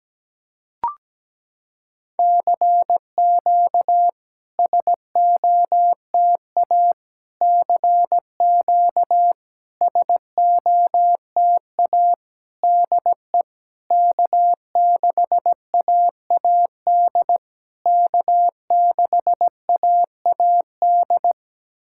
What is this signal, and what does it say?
Morse code, heard as a keyed tone, about 17 wpm, 700 Hz. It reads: CQ SOTA CQ SOTA DE K6AAD K6AAD
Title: SOTA QSO Practice 17wpm